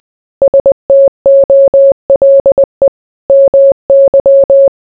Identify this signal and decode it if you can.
Morse code, heard as a keyed tone, about 20 wpm, 550 Hz. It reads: STOLE MY